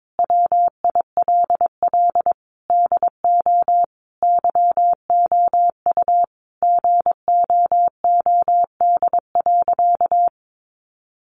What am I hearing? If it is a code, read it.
Morse code, heard as a keyed tone, about 22 wpm, 700 Hz. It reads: WILL DO YOU GOOD.
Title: Sentences from Top 1200 Words 22wpm